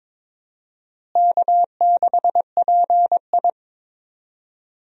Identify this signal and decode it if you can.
Morse code, heard as a keyed tone, about 22 wpm, 700 Hz. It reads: K6PI
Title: USA Call Signs - 4 Characters - Repeated 2x - 22wpm